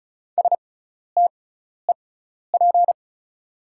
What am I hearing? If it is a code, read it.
Morse code, heard as a keyed tone, about 36 wpm, spaced as for 12 wpm, 690 Hz. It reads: STEP
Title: 6x Character Spacing - Top 500 Words - 35wpm